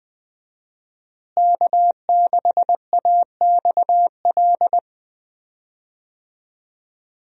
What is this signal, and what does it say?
Morse code, heard as a keyed tone, about 20 wpm, 700 Hz. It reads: K6AXL
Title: USA Call Signs - 5 Characters - Repeated 3x - 20wpm